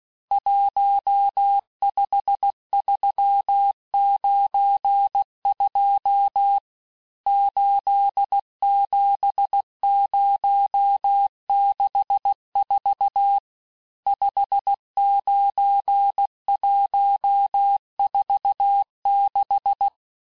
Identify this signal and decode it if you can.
Morse code, heard as a keyed tone, about 16 wpm, 800 Hz. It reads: 15392 87064 59146
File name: cours-du-lundi files / Cours-1586-texte-chiffre-vitesse-15mpm